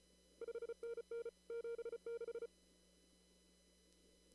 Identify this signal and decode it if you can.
Morse code, heard as a keyed tone, about 34 wpm, 470 Hz. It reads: 5NN 76
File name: CW / kt5j wpx cw-op-k5pi